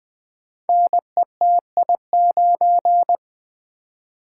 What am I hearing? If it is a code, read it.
Morse code, heard as a keyed tone, about 20 wpm, 700 Hz. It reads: NETI9
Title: Code Groups - 5 Letter:Numbers - 20wpm